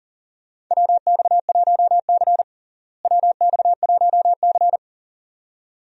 Morse code, heard as a keyed tone, about 40 wpm, 700 Hz. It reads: WX1C WX1C
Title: USA Call Signs - 4 Characters - Repeated 2x - 40wpm